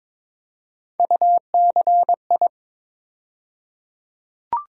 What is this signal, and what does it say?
Morse code, heard as a keyed tone, about 22 wpm, 700 Hz. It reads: UCI
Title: USA Call Sign Postfixes - 3 Letters 22wpm